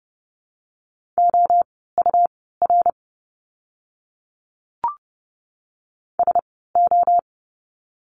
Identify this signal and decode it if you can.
Morse code, heard as a keyed tone, about 30 wpm, spaced as for 17 wpm, 700 Hz. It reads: OUR SO
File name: course / 30:10wpm / 230 Lesson 37f - Letter x - Words 30wpm